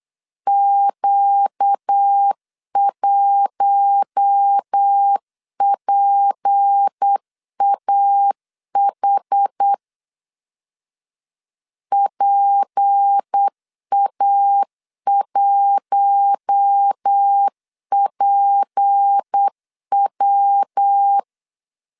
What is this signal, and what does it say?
Morse code, heard as a keyed tone, about 8 wpm, 800 Hz. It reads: Q1PAH PA1PW
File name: LEZIONI / 6 PKX0 / 6 P 8wpm 3025